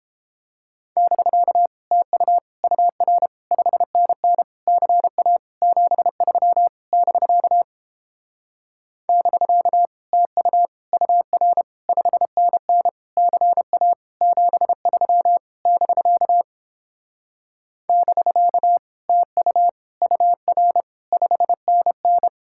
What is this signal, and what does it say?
Morse code, heard as a keyed tone, about 31 wpm, 700 Hz. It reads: <BK> TU UR 5NN CA 73 <BK> <BK> TU UR 5NN CA 73 <BK> <BK> TU UR 5NN